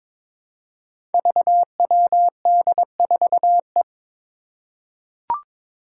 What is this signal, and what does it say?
Morse code, heard as a keyed tone, about 22 wpm, 700 Hz. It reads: VWD4E
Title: Five Challenging Letter-Number Combinations 22wpm